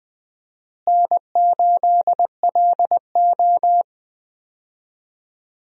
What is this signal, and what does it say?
Morse code, heard as a keyed tone, about 20 wpm, 700 Hz. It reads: N8LO
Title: USA Call Signs - 4 Characters - Repeated 2x - 20wpm